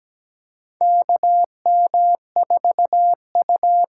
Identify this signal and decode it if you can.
Morse code, heard as a keyed tone, about 17 wpm, 700 Hz. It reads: KM4U
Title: USA Call Signs - 6 Characters - 8x Character Spacing - 17wpm